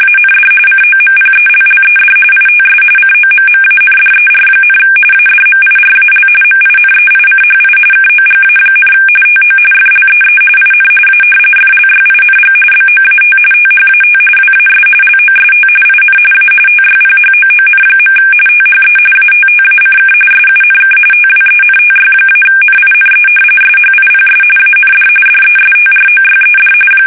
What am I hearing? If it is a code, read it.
RTTY, about 45.45 baud. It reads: AND COMPELLED ITS COMMANDER, ADMIRAL MINEICHI KOGA, TO FLEE IN A FLYING BOAT THAT WAS LOST IN A TYPHOON. THE ATTACK SUNK TWENTY-NINE SHIPS AND DESTROYED
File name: fsk850 / battleshipnj